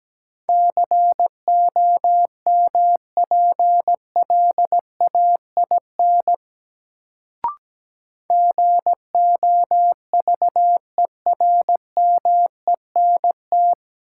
Morse code, heard as a keyed tone, about 17 wpm, 700 Hz. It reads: COMPLAIN GOVERMENT
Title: Sentences from Top 1300 Words - New Word Drill Alt - 17wpm